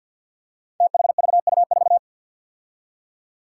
Morse code, heard as a keyed tone, about 51 wpm, 700 Hz. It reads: THVU4
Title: Five Challenging Letter-Number Combinations 50wpm